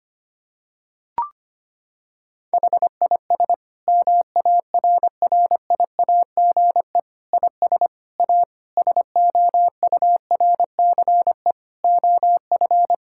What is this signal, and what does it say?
Morse code, heard as a keyed tone, about 25 wpm, 700 Hz. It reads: HIS MARRIAGE IS A SOURCE OF
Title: Sentences from Top 1000 Words - Review Part 6 - 25wpm